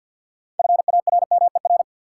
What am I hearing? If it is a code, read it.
Morse code, heard as a keyed tone, about 52 wpm, 700 Hz. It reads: FARMER
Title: Six Letter Words - 50wpm